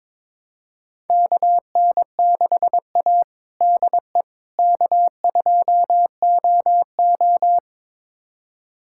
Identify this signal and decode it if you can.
Morse code, heard as a keyed tone, about 21 wpm, 700 Hz. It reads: KN6A DE K2OO
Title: USA Callsign DE Callsign - 4 Characters - 22wpm